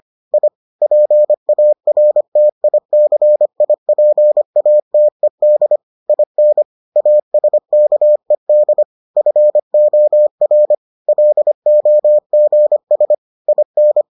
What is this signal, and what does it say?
Morse code, heard as a keyed tone, about 25 wpm, 600 Hz. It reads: I PARTICIPATED IN ASKED FOR LOGS IN